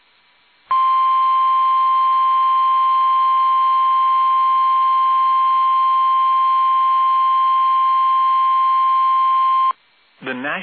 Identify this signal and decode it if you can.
Emergency Alert System (SAME) tones (the 1050 Hz attention tone), with speech. It is heard from 0.6 to 9.7 s.